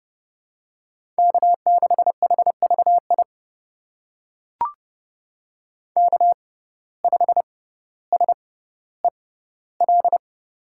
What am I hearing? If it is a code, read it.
Morse code, heard as a keyed tone, about 30 wpm, spaced as for 15 wpm, 700 Hz. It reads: K6HVI K5SEL
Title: USA Call Signs - 5 Characters - 6x Character Spacing - 30wpm